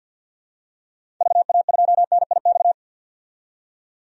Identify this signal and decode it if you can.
Morse code, heard as a keyed tone, about 51 wpm, 700 Hz. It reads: VA2NIX
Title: DXCC-100 Call Signs - 6 Characters - 50wpm